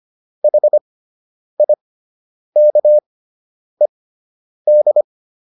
Morse code, heard as a keyed tone, about 25 wpm, spaced as for 9 wpm, 600 Hz. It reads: HIKED
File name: ED5 09